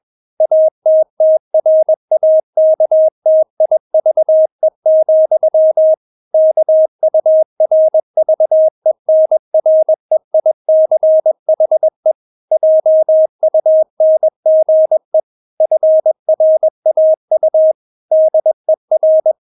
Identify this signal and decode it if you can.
Morse code, heard as a keyed tone, about 21 wpm, 630 Hz. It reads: ATTRAKTIVE, KURVENREICHE JUNGE FRAU DER
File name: Witz2 21wpm